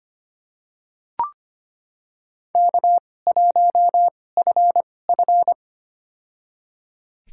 Morse code, heard as a keyed tone, about 25 wpm, spaced as for 18 wpm, 700 Hz. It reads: K1FF
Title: USA Call Signs - 4 Characters - 2x Character Spacing - 25wpm